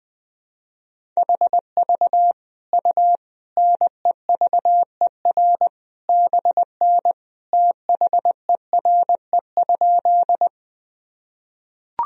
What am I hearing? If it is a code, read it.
Morse code, heard as a keyed tone, about 20 wpm, 700 Hz. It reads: HV U NEVER BN THERE?
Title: Sentences from Top 300 Words and Abbreviations 20wpm